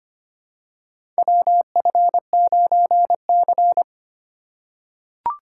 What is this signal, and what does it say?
Morse code, heard as a keyed tone, about 25 wpm, 700 Hz. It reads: WF9C
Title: USA Call Signs - 4 Characters - 3x Character Spacing - 25wpm